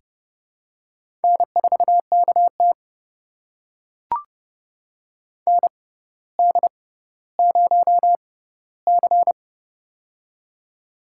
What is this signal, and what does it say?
Morse code, heard as a keyed tone, about 30 wpm, spaced as for 15 wpm, 700 Hz. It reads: N4KT ND0C
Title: USA Call Signs - 4 Characters - 6x Character Spacing - 30wpm